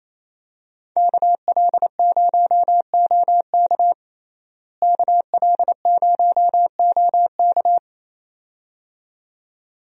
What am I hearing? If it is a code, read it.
Morse code, heard as a keyed tone, about 28 wpm, 700 Hz. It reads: KL0OK KL0OK
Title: USA Call Signs - 5 Characters - Repeated 2x - 28wpm